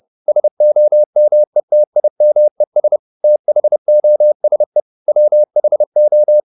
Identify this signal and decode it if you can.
Morse code, heard as a keyed tone, about 30 wpm, 600 Hz. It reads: SOMETIMES THOSE WHO
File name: the old man and the sea / the old man and the sea 1 30wpm